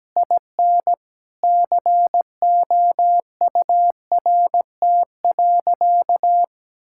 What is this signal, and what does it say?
Morse code, heard as a keyed tone, about 17 wpm, 700 Hz. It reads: IN COURT.
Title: Sentences from Top 1600 Words 17wpm